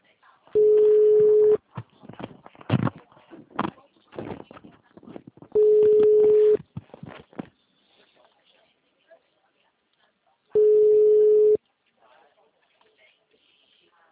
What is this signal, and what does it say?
Ringback tone (Europe, 425 Hz), with speech.